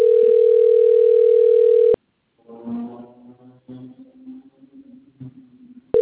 Ringback tone (North America), 440 + 480 Hz.